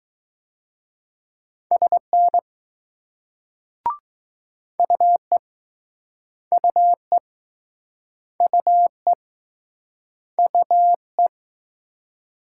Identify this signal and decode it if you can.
Morse code, heard as a keyed tone, about 20 wpm, 700 Hz. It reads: SN UE UE UE UE